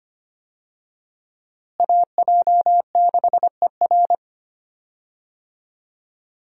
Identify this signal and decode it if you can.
Morse code, heard as a keyed tone, about 25 wpm, 700 Hz. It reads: AJ6ER
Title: USA Call Signs - 5 Characters - Repeated 3x - 25wpm